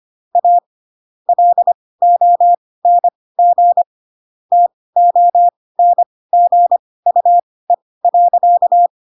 Morse code, heard as a keyed tone, about 25 wpm, spaced as for 18 wpm, 700 Hz. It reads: A LONG TONGUE.
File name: Pinocchio ch3 12 25 ik1hge